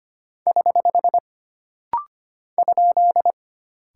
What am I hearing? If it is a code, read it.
Morse code, heard as a keyed tone, about 25 wpm, 700 Hz. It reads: <HH> ?
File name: Single Punctuation-Prosign - Rapid-Fire - 25wpm